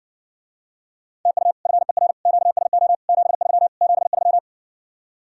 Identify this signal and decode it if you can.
Morse code, heard as a keyed tone, about 70 wpm, 700 Hz. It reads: TU FER QSO 73 73